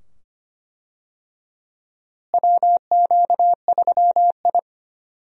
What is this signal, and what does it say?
Morse code, heard as a keyed tone, about 25 wpm, 700 Hz. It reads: WQ3I
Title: USA Call Signs - 4 Characters - Repeated 2x - 25wpm